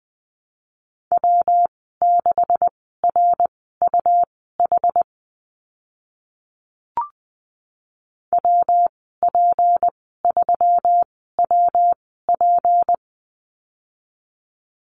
Morse code, heard as a keyed tone, about 20 wpm, spaced as for 14 wpm, 700 Hz. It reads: W6RUH WP3WP
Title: 193 Lesson 31g - Letter v - Call Signs 20wpm